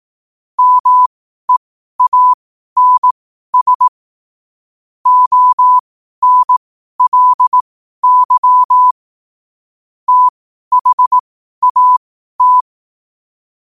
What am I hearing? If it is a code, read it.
Morse code, heard as a keyed tone, about 18 wpm, spaced as for 12 wpm, 1000 Hz. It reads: MEANS ONLY THAT
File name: qotdfiles / 12wpm